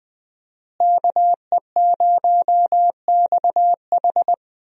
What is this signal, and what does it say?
Morse code, heard as a keyed tone, about 20 wpm, 700 Hz. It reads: KE0XH
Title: USA Call Signs - 5 Characters - Repeated 2x - 20wpm